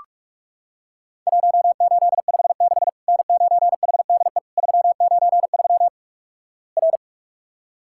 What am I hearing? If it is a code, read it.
Morse code, heard as a keyed tone, about 44 wpm, 700 Hz. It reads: 1856 N9HDE 393 R
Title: Worked All Europe DX Contest - QTC Exchange - 45wpm